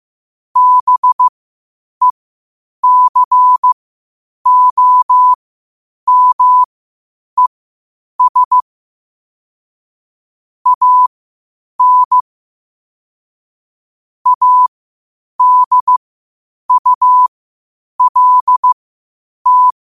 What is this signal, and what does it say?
Morse code, heard as a keyed tone, about 15 wpm, spaced as for 8 wpm, 1000 Hz. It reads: BECOMES AN ADULT